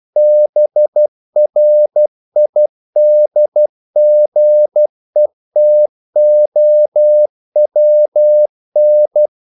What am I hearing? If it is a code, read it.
Morse code, heard as a keyed tone, about 12 wpm, 600 Hz. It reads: BRIDGETOWN
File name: CW / Practice / NationalCapitals12